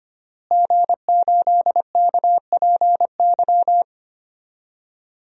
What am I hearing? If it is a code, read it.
Morse code, heard as a keyed tone, about 25 wpm, 700 Hz. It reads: G8KPY
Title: DXCC-100 Call Signs - 5 Characters - Repeated 2x - 25wpm